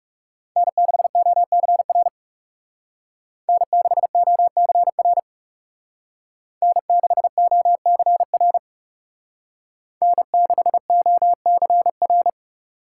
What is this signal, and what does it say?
Morse code, heard as a keyed tone, about 36 wpm, 700 Hz. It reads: N6OCR N6OCR N6OCR N6OCR
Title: Speed Racing - USA 1x3 Call Signs 30wpm